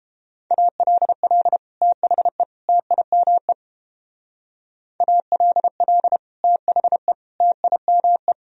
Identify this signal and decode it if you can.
Morse code, heard as a keyed tone, about 29 wpm, 700 Hz. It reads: ALL THE TIME ALL THE TIME